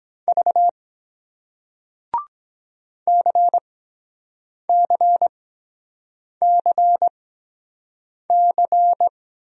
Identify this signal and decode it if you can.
Morse code, heard as a keyed tone, about 23 wpm, spaced as for 7 wpm, 700 Hz. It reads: V CCCC